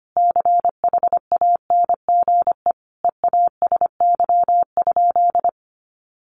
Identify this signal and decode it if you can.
Morse code, heard as a keyed tone, about 25 wpm, 700 Hz. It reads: CHANGE EASY?
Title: Sentences from Top 1500 Words 25wpm